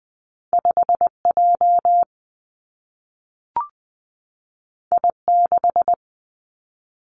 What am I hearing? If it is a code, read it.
Morse code, heard as a keyed tone, about 20 wpm, 700 Hz. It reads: 5 J I 6
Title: Two Challenging Letter-Number Combinations 20wpm